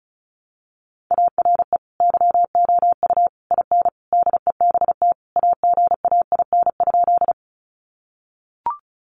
Morse code, heard as a keyed tone, about 34 wpm, 700 Hz. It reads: ARE YOU IN DEBT AGAIN?
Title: Sentences from Top 1300 Words 35wpm